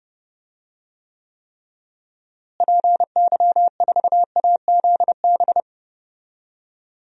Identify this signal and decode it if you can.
Morse code, heard as a keyed tone, about 30 wpm, 700 Hz. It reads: PY4AZB
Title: DXCC-100 Call Signs - 6 Characters - Repeated 3x - 30wpm